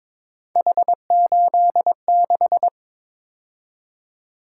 Morse code, heard as a keyed tone, about 23 wpm, 700 Hz. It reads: H86
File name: practice / 22wpm / Three Letter-Number Combinations - 22wpm